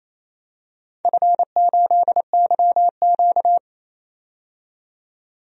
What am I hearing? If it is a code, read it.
Morse code, heard as a keyed tone, about 28 wpm, 700 Hz. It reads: F8YQ
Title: DXCC-100 Call Signs - 4 Characters - 28wpm